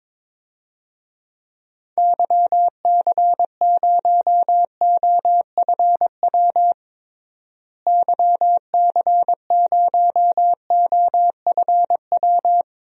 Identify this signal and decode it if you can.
Morse code, heard as a keyed tone, about 22 wpm, 700 Hz. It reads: YC0OFW YC0OFW
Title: DXCC-100 Call Signs - 6 Characters - Repeated 3x - 22wpm